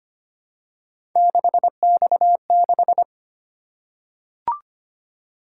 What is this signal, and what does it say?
Morse code, heard as a keyed tone, about 25 wpm, 700 Hz. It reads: 6X6
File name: Three Letter-Number Combinations - 25wpm